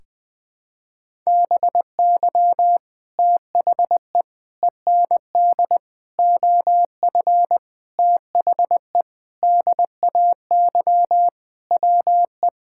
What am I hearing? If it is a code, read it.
Morse code, heard as a keyed tone, about 20 wpm, 700 Hz. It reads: BY THE END OF THE DAY WE
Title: Sentences with 4-Word Phrases - 20wpm